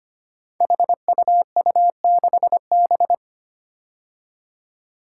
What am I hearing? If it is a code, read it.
Morse code, heard as a keyed tone, about 25 wpm, 700 Hz. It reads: HUU6B
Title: Code Groups - 5 Letter:Numbers - 25wpm